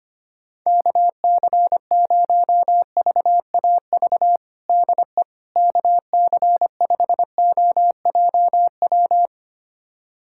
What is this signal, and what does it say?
Morse code, heard as a keyed tone, about 25 wpm, 700 Hz. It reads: KC0VAV DE KC5OJW
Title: USA Callsign DE Callsign 25wpm